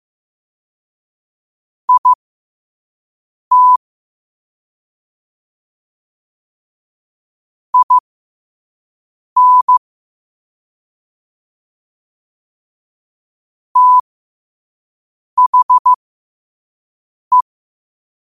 Morse code, heard as a keyed tone, about 15 wpm, spaced as for 5 wpm, 1000 Hz. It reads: IT IN THE
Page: Quotes for Wed, 13 Aug 2025 in Morse Code at 5 words per minute.